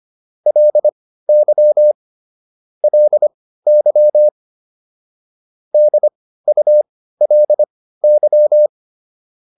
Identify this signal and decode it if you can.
Morse code, heard as a keyed tone, about 25 wpm, spaced as for 15 wpm, 600 Hz. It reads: LY LY DULY